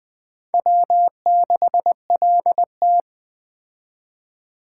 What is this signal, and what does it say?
Morse code, heard as a keyed tone, about 20 wpm, 700 Hz. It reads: W6LT
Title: Speed Racing - USA 1x2 Call Signs 17wpm